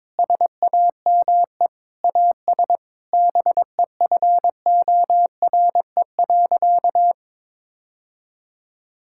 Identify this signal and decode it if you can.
Morse code, heard as a keyed tone, about 22 wpm, 700 Hz. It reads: SAME AS BEFORE.
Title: Sentences from Top 1000 Words - Review Part 5 - 22wpm